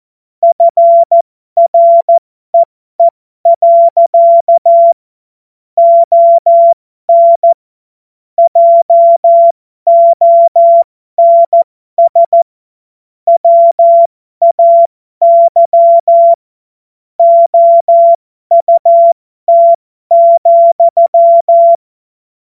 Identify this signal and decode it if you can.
Morse code, heard as a keyed tone, about 14 wpm, spaced as for 12 wpm, 680 Hz. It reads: FREE. ON JONS WAY OUT,